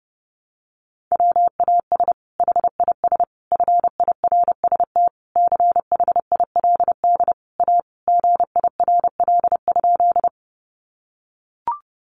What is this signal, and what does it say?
Morse code, heard as a keyed tone, about 30 wpm, 700 Hz. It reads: WAS HIS FIRST CHILD A GIRL?
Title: Sentences from Top 800 Words 30wpm